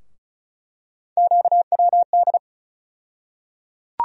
Morse code, heard as a keyed tone, about 34 wpm, 700 Hz. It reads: QWD
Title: USA Call Sign Postfixes - 3 Letters 35wpm